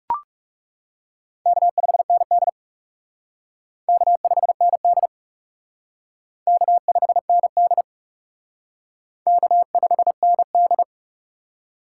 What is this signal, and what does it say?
Morse code, heard as a keyed tone, about 37 wpm, 700 Hz. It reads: K5ND K5ND K5ND K5ND
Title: Speed Racing - USA Call Signs 30wpm